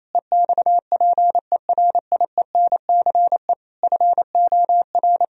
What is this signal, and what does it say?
Morse code, heard as a keyed tone, about 28 wpm, 700 Hz. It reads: EXPERIENCE FOR
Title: Sentences from Top 1000 Words - Review Part 4 - 28wpm